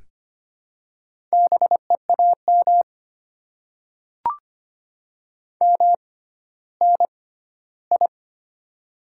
Morse code, heard as a keyed tone, about 25 wpm, spaced as for 14 wpm, 700 Hz. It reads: BEAM MNI
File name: Top 90 QSO Elements - 6x Character Spacing - 25wpm